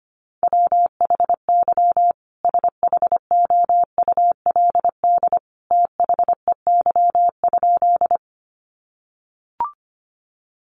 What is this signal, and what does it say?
Morse code, heard as a keyed tone, about 25 wpm, 700 Hz. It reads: WHY SHOULD THEY?
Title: Sentences from Top 300 Words 25wpm